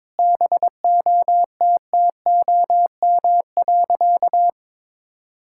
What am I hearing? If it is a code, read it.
Morse code, heard as a keyed tone, about 22 wpm, 700 Hz. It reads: BOTTOM.
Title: Sentences from Top 1000 Words - Review Part 6 - 22wpm